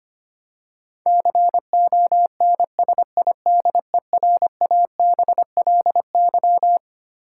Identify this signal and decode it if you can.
Morse code, heard as a keyed tone, about 25 wpm, 700 Hz. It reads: CONSIDERABLY
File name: practice / 25wpm / Twelve Letter Words - 25wpm